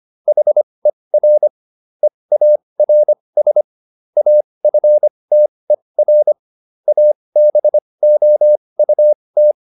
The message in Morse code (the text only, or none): HER EARS AFTER ABOUT